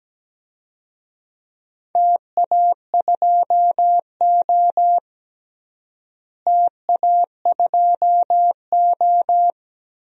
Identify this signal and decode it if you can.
Morse code, heard as a keyed tone, about 17 wpm, 700 Hz. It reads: TA2O TA2O
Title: DXCC-100 Call Signs - 4 Characters - Repeated 3x - 17wpm